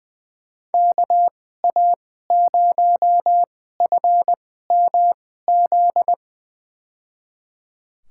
Morse code, heard as a keyed tone, about 20 wpm, spaced as for 14 wpm, 700 Hz. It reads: KA0FMZ
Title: USA Call Signs - 6 Characters - 2x Character Spacing - 20wpm